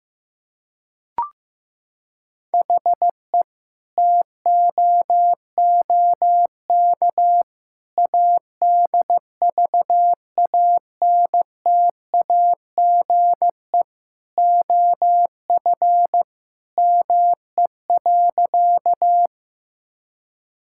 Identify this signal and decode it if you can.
Morse code, heard as a keyed tone, about 15 wpm, 700 Hz. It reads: HE TOOK ADVANTAGE OF ME.
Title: Sentences from Top 1000 Words - Review Part 6 - 15wpm